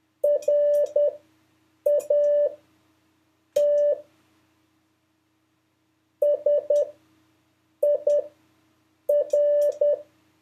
Morse code, heard as a keyed tone, about 10 wpm, spaced as for 6 wpm, 580 Hz. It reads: RAT SIR
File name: 5of5-test